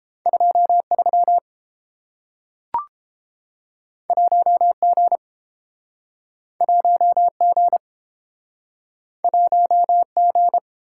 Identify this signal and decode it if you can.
Morse code, heard as a keyed tone, about 30 wpm, 700 Hz. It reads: 23 1G 1G 1G